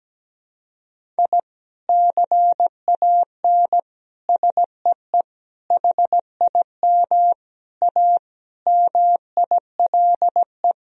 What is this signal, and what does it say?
Morse code, heard as a keyed tone, about 17 wpm, 700 Hz. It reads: I CAN SEE HIM A MILE
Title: Sentences from Top 1000 Words - Review Part 1 - 17wpm